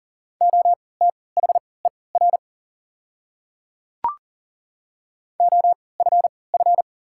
Morse code, heard as a keyed tone, about 40 wpm, spaced as for 23 wpm, 700 Hz. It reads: OTHER OFF
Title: 3x Character Spacing - Top 200 Words - 40wpm